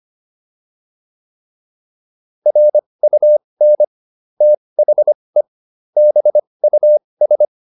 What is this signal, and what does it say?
Morse code, heard as a keyed tone, about 25 wpm, spaced as for 20 wpm, 600 Hz. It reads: RUN THE BUS